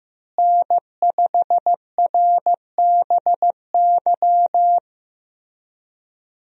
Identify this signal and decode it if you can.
Morse code, heard as a keyed tone, about 15 wpm, 700 Hz. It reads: N5RBY
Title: USA Call Signs - 5 Characters - Repeated 3x - 15wpm